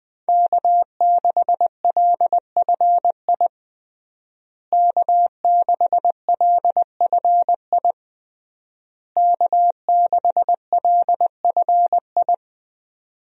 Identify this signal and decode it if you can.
Morse code, heard as a keyed tone, about 20 wpm, 700 Hz. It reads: K6LFI K6LFI K6LFI
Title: USA Call Signs - 5 Characters - Repeated 3x - 20wpm